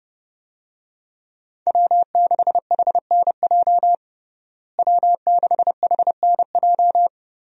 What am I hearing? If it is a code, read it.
Morse code, heard as a keyed tone, about 30 wpm, 700 Hz. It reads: W6HNJ W6HNJ
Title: USA Call Signs - 5 Characters - Repeated 3x - 30wpm